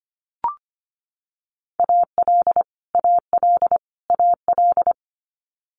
Morse code, heard as a keyed tone, about 25 wpm, 700 Hz. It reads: AL AL AL
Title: Multi-letter Phonemes Repeated 3x 25wpm